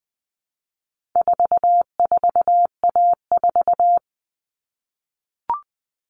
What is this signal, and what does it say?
Morse code, heard as a keyed tone, about 20 wpm, 700 Hz. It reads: 44A4
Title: All Cut Numbers - 4-Digit 20wpm